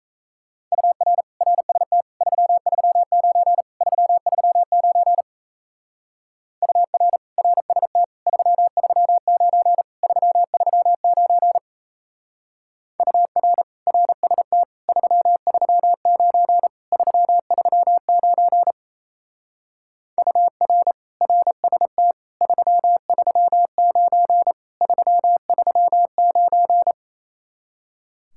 Morse code, heard as a keyed tone, about 34 wpm, 700 Hz. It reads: UR RST 339 339 UR RST 339 339 UR RST 339 339 UR RST 339 339
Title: Speed-Racing Common QSO Phrases - 28wpm